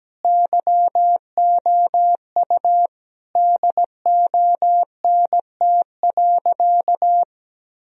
Morse code, heard as a keyed tone, about 17 wpm, 700 Hz. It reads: YOU DONT.
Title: Sentences from Top 1000 Words - Review Part 4 - 17wpm